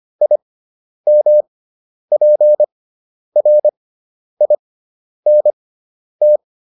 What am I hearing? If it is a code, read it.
Morse code, heard as a keyed tone, about 25 wpm, spaced as for 10 wpm, 600 Hz. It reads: IMPRINT